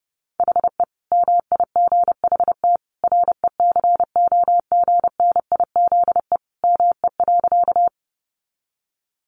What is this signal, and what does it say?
Morse code, heard as a keyed tone, about 30 wpm, 700 Hz. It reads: HE MIGHT RECOGNIZE ME.
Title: Sentences from Top 1400 Words 30wpm